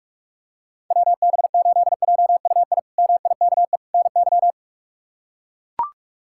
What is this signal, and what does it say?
Morse code, heard as a keyed tone, about 42 wpm, 700 Hz. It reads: WB8JUI MIKE NY
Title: K1USN SST QSO Practice - 45wpm